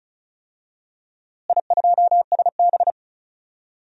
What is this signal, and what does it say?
Morse code, heard as a keyed tone, about 36 wpm, 700 Hz. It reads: I2SB
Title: DXCC-100 Call Signs - 4 Characters - Repeated 3x - 35wpm